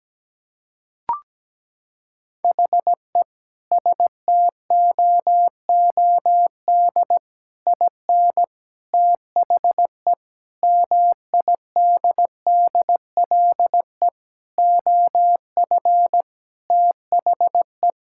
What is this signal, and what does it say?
Morse code, heard as a keyed tone, about 17 wpm, 700 Hz. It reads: HE STOOD IN THE MIDDLE OF THE
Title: Sentences from Top 1000 Words - Review Part 6 - 17wpm